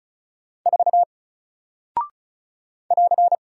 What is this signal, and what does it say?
Morse code, heard as a keyed tone, about 36 wpm, 700 Hz. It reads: 4 <AR>
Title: Single Character (Letter:Number:Punctuation:Prosign) Alt 35wpm